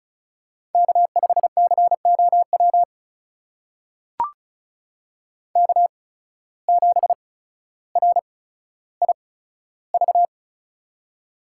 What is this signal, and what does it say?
Morse code, heard as a keyed tone, about 36 wpm, spaced as for 15 wpm, 700 Hz. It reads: K5COW K7RIV
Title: USA Call Signs - 5 Characters - 8x Character Spacing - 35wpm